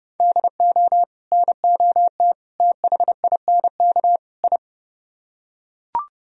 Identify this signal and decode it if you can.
Morse code, heard as a keyed tone, about 28 wpm, 700 Hz. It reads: DO NOT THINK I